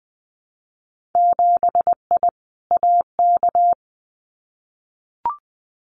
Morse code, heard as a keyed tone, about 20 wpm, 700 Hz. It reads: 7I AK
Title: Winter Field Day Exchanges Encore 20wpm